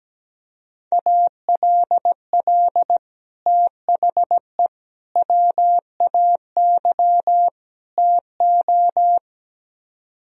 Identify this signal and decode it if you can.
Morse code, heard as a keyed tone, about 17 wpm, 700 Hz. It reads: ALL THE WAY TO
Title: Speed-Racing - Sets of 4 Words 17wpm